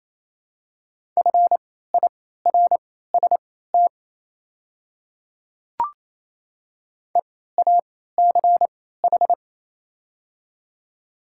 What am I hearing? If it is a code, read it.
Morse code, heard as a keyed tone, about 28 wpm, spaced as for 16 wpm, 700 Hz. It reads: FIRST EACH
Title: 3x Character Spacing - Top 300 Words - 28wpm